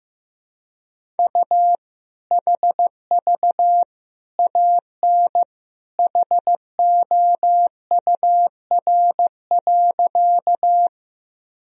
Morse code, heard as a keyed tone, about 15 wpm, 700 Hz. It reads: U HV AN HOUR.
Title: Sentences from Top 400 Words and Abbreviations 15wpm